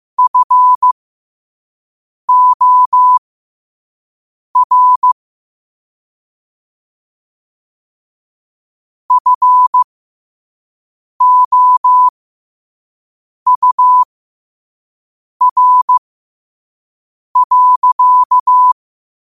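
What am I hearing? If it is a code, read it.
Morse code, heard as a keyed tone, about 15 wpm, spaced as for 5 wpm, 1000 Hz. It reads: FOR FOUR.
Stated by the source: Morse quote-of-the-day practice at 5 wpm.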